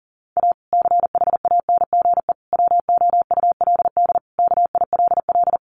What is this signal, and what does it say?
Morse code, heard as a keyed tone, about 40 wpm, 700 Hz. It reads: A CHANGE WOULD KILL
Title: Sentences from Top 1400 Words 40wpm